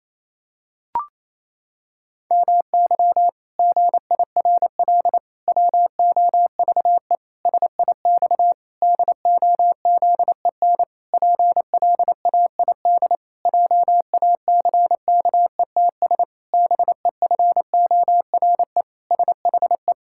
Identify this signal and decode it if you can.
Morse code, heard as a keyed tone, about 28 wpm, 700 Hz. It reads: MY GIRL WOVE SIX DOZEN PLAID JACKETS BEFORE SHE
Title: Pangram Challenge v2 - 28wpm